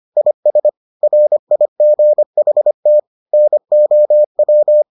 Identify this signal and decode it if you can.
Morse code, heard as a keyed tone, about 25 wpm, 600 Hz. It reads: IS RIGHT NOW